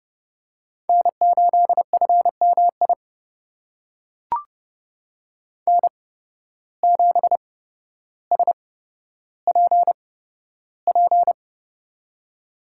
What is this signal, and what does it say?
Morse code, heard as a keyed tone, about 30 wpm, spaced as for 13 wpm, 700 Hz. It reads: N8FMI N7SPP
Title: USA Call Signs - 5 Characters - 8x Character Spacing - 30wpm